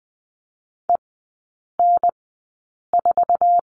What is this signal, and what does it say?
Morse code, heard as a keyed tone, about 20 wpm, spaced as for 8 wpm, 700 Hz. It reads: EN4
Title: 047 Lesson 9e - Number 4 - Three Characters 20wpm